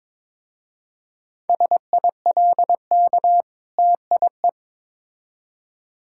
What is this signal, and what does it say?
Morse code, heard as a keyed tone, about 22 wpm, 690 Hz. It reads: SILK TIE
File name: Difficult Words with Hints - 22wpm